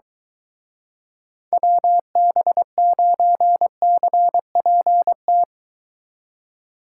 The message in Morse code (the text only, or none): WB9CPT